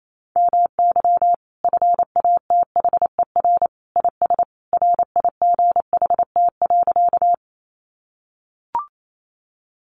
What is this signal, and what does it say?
Morse code, heard as a keyed tone, about 28 wpm, 700 Hz. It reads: MY FATHER IS RIGHT.
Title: Sentences from Top 300 Words 28wpm